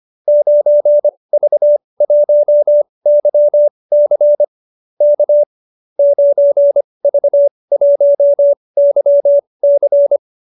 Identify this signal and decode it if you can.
Morse code, heard as a keyed tone, about 25 wpm, spaced as for 20 wpm, 590 Hz. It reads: 9V1YC K 9V1YC